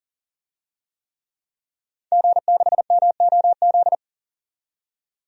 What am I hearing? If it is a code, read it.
Morse code, heard as a keyed tone, about 40 wpm, 700 Hz. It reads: G6MOZ
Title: DXCC-100 Call Signs - 5 Characters - 40wpm